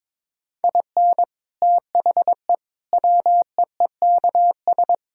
Morse code, heard as a keyed tone, about 22 wpm, 700 Hz. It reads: IN THE WEEKS
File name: practice / 22wpm / Sets of 7 Words - 22wpm